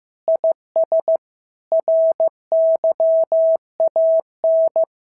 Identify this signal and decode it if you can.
Morse code, heard as a keyed tone, about 14 wpm, 650 Hz. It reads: IS RYAN
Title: SOTA QSO Practice 15wpm